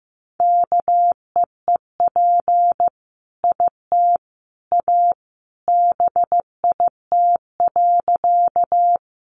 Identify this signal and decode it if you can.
Morse code, heard as a keyed tone, about 14 wpm, 700 Hz. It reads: KEEP IT A BIT.
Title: Sentences from Top 900 Words 15wpm